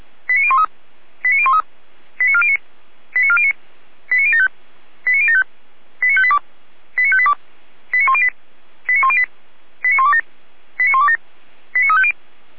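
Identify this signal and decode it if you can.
Five-tone selective call (ZVEI), heard as quick runs of five tones, tones of about 70 ms. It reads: 89013 89013 89409 89409 89075 89075 89582 89582 89108 89108 89127 89127 89350
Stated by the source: ZVEI five-tone call 89370 received by SDR.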